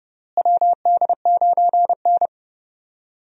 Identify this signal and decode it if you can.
Morse code, heard as a keyed tone, about 32 wpm, 700 Hz. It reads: WD9N
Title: USA Call Signs - 4 Characters - Repeated 3x - 30wpm